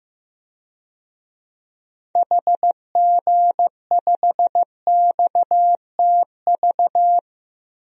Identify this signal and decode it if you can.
Morse code, heard as a keyed tone, about 16 wpm, 700 Hz. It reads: HG5XTV
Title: DXCC-100 Call Signs - 6 Characters - Repeated 2x - 15wpm